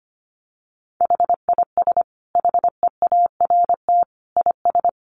Morse code, heard as a keyed tone, about 25 wpm, 700 Hz. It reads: HIS HEART IS
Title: Sentences from Top 1500 Words 25wpm